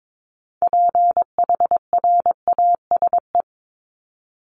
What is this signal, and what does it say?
Morse code, heard as a keyed tone, about 22 wpm, 700 Hz. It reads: PHRASE
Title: Top 1501-1750 Words 22wpm